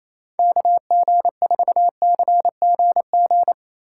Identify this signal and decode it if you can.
Morse code, heard as a keyed tone, about 28 wpm, 700 Hz. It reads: KG4CGG
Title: USA Call Signs - 3x Character Spacing - 28wpm